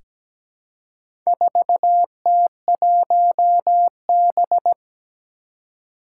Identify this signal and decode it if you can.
Morse code, heard as a keyed tone, about 17 wpm, 700 Hz. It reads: 4T1B
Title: Four Challenging Letter-Number Combinations 17wpm